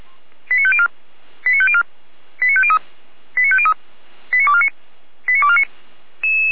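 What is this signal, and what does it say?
Five-tone selective call (ZVEI), heard as quick runs of five tones, tones of about 80 ms. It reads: 89504 89504 89503 89503 89249 89249
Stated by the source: ZVEI five-tone selective call received via SDR.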